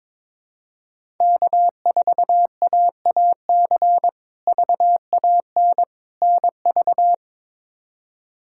Morse code, heard as a keyed tone, about 22 wpm, 700 Hz. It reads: K4AAC VAN NV
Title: K1USN SST QSO Practice - 22wpm